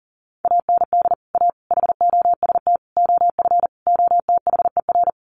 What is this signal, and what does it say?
Morse code, heard as a keyed tone, about 40 wpm, 700 Hz. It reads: AND A HOST OF OTHER
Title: Sets of 5 Words Encore2 40wpm